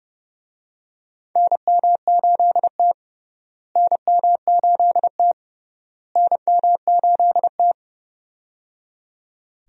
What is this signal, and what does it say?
Morse code, heard as a keyed tone, about 30 wpm, 700 Hz. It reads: NM8T NM8T NM8T
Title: USA Call Signs - 4 Characters - Repeated 3x - 30wpm